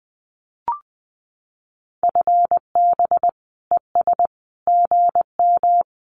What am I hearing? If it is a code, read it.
Morse code, heard as a keyed tone, about 20 wpm, 700 Hz. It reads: FB ES GM
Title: Common QSO Phrases Encore 20wpm